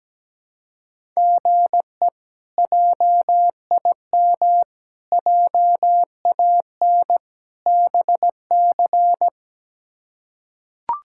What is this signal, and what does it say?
Morse code, heard as a keyed tone, about 17 wpm, 700 Hz. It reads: GE JIM JAN BC
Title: K1USN SST QSO Practice - 17wpm